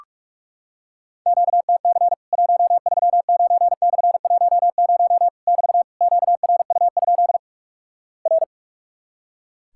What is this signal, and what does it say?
Morse code, heard as a keyed tone, about 46 wpm, 700 Hz. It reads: QTC 139/10 <BT> QRU? R
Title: Worked All Europe DX Contest - QTC Exchange - 45wpm